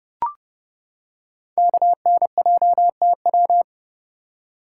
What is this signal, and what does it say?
Morse code, heard as a keyed tone, about 30 wpm, 700 Hz. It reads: KNJTW
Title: Code Groups - 5 Letter:Numbers - 30wpm